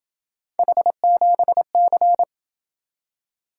Morse code, heard as a keyed tone, about 26 wpm, 700 Hz. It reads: H7C